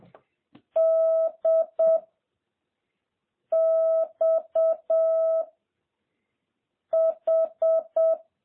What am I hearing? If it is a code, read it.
Morse code, heard as a keyed tone, about 7 wpm, 650 Hz. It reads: D X H
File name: jh8cbh / 2021cw